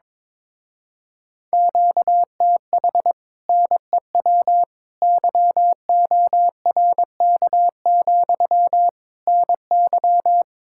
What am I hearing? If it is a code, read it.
Morse code, heard as a keyed tone, about 22 wpm, 700 Hz. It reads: QTH NEW YORK, NY
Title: Speed-Racing Common QSO Phrases Encore - 22wpm